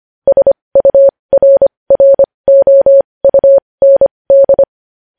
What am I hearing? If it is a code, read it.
Morse code, heard as a keyed tone, about 25 wpm, spaced as for 20 wpm, 550 Hz. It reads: SURROUND
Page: News Headlines in Morse Code at 20 WPM